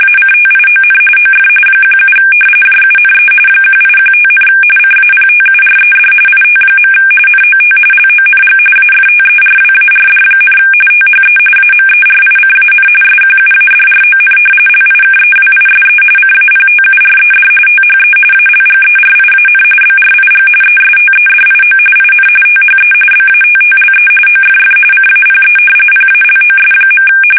RTTY, about 45.45 baud. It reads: ACTION OF 17 JUNE 1815, CAPTURED IT. NOT LONG AFTERWARD, THE AMERICAN SQUADRON LIKEWISE CAPTURED THE ALGERIAN BRIG ESTEDIO IN THE BATTLE OFF CAPE PALOS.
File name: fsk850 / usnavy2